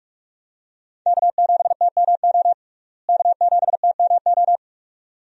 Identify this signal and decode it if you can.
Morse code, heard as a keyed tone, about 47 wpm, 700 Hz. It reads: K7TMO K7TMO
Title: USA Call Signs - 5 Characters - Repeated 2x - 45wpm